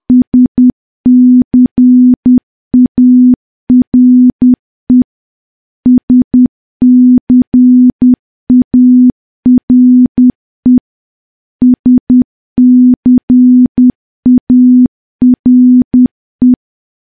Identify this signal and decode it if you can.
Morse code, heard as a keyed tone, about 10 wpm, 250 Hz. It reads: SCARE SCARE SCARE
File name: morse / morse-training-10WPM